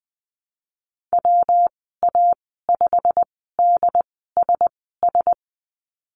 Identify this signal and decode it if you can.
Morse code, heard as a keyed tone, about 20 wpm, spaced as for 15 wpm, 700 Hz. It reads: WA5DSS
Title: 122 Lesson 20g - Number 3 - Call Signs 20wpm